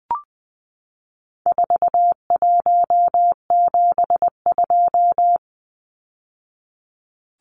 Morse code, heard as a keyed tone, about 20 wpm, 700 Hz. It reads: 4172
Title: Numbers - 4-Digit 20wpm